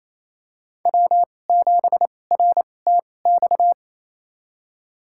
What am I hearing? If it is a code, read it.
Morse code, heard as a keyed tone, about 28 wpm, spaced as for 20 wpm, 700 Hz. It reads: W7RTX
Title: USA Call Signs - 5 Characters - 2x Character Spacing - 28wpm